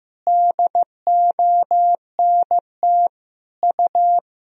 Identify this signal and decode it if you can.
Morse code, heard as a keyed tone, about 15 wpm, 700 Hz. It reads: DONT U
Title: Sentences from Top 300 Words and Abbreviations 15wpm